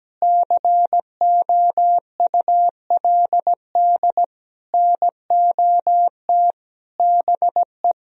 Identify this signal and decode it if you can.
Morse code, heard as a keyed tone, about 17 wpm, 700 Hz. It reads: COULD NOT BE
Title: Sentences from Top 1000 Words - Review Part 1 - 17wpm